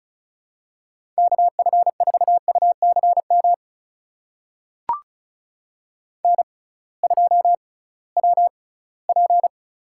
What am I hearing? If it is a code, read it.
Morse code, heard as a keyed tone, about 36 wpm, spaced as for 20 wpm, 700 Hz. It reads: KF4UCM N2WP
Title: USA Call Signs - 6x Character Spacing - 35wpm